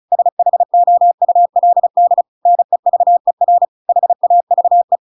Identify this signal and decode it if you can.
Morse code, heard as a keyed tone, about 35 wpm, 700 Hz. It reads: SHOULD NEVER HAVE
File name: Pinocchio / Pinocchio ch7 35 35 ik1hge